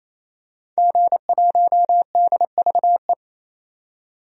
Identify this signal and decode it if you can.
Morse code, heard as a keyed tone, about 28 wpm, 700 Hz. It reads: G1DVE
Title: DXCC-100 Call Signs - 5 Characters - 28wpm